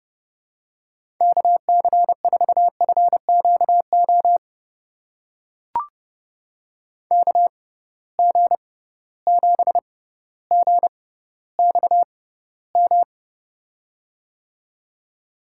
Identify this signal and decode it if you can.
Morse code, heard as a keyed tone, about 31 wpm, spaced as for 15 wpm, 700 Hz. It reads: KC4FQO KG7GXM
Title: USA Call Signs - 6 Characters - 6x Character Spacing - 30wpm